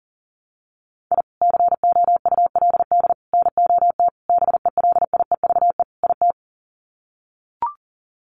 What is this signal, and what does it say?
Morse code, heard as a keyed tone, about 40 wpm, 700 Hz. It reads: I COULD NOT BELIEVE IT
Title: Sets of 5 Words Encore 40wpm